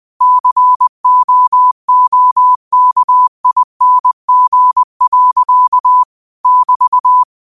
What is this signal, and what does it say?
Morse code, heard as a keyed tone, about 20 wpm, 1000 Hz. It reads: COOKING. <BT>
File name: qotdfiles / 20wpm